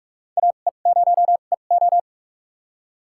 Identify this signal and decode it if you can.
Morse code, heard as a keyed tone, about 46 wpm, spaced as for 32 wpm, 700 Hz. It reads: AE0EO
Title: USA Call Signs - 5 Characters - 2x Character Spacing - 45wpm